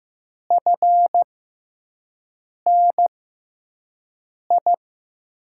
Morse code, heard as a keyed tone, about 15 wpm, spaced as for 5 wpm, 700 Hz. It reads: FNI
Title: USA Call Signs - 5 Characters - 6x Character Spacing - 15wpm